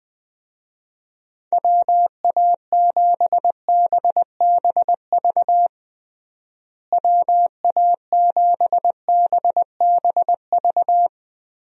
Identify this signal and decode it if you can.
Morse code, heard as a keyed tone, about 21 wpm, 700 Hz. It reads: WA7BBV WA7BBV
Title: USA Call Signs - 6 Characters - Repeated 3x - 20wpm